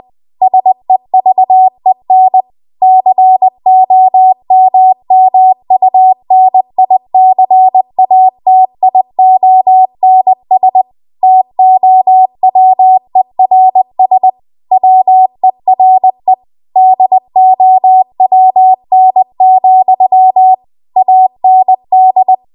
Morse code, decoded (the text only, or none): SEVEN COMMUNICATIONS TOWERS WERE DOWN, AND